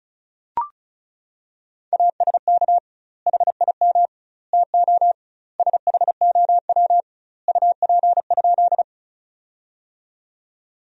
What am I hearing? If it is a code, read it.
Morse code, heard as a keyed tone, about 34 wpm, 700 Hz. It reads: ASK HIM TO SHOW UP?
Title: 2x Word-Spacing - Sentences from Top 200 Words - 35wpm